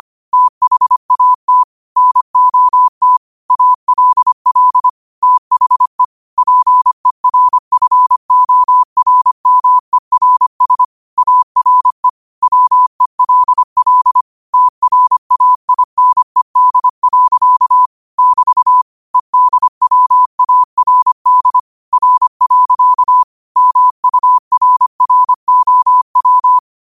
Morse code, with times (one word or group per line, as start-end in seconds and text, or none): 0.3-1.6 THAT
2.0-3.2 NOT
3.5-4.9 ALL
5.2-6.0 THE
6.4-10.8 PERFORMERS
11.2-12.1 ARE
12.4-14.2 WELL
14.5-17.9 TRAINED.
18.2-18.8 <BT>
19.1-21.6 EDWARD
21.9-23.2 R.
23.6-26.6 MURROW